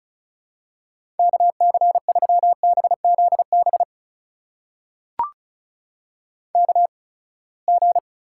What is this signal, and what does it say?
Morse code, heard as a keyed tone, about 36 wpm, spaced as for 24 wpm, 700 Hz. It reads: KC3BZB KG
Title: USA Call Signs - 6 Characters - 8x Character Spacing - 35wpm